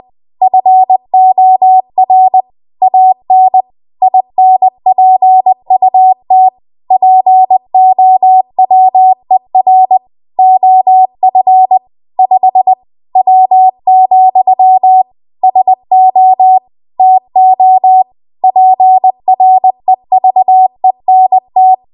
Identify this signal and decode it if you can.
Morse code, heard as a keyed tone, about 20 wpm, 750 Hz. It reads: FOR AN INPUT POWER OF 5 W, SO TO PREVENT
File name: Archive / 20 WPM / 20WPM